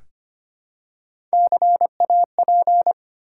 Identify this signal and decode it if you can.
Morse code, heard as a keyed tone, about 25 wpm, 690 Hz. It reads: CAP
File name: practice / 25wpm / Easy Words with Hints - 25wpm